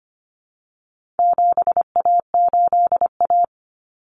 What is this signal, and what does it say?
Morse code, heard as a keyed tone, about 25 wpm, 700 Hz. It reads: 7A8A
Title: Common Cut Numbers - 4-Digit 25wpm